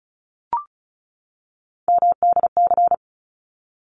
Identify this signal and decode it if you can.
Morse code, heard as a keyed tone, about 34 wpm, 700 Hz. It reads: MDC
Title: ARRL Sections 35wpm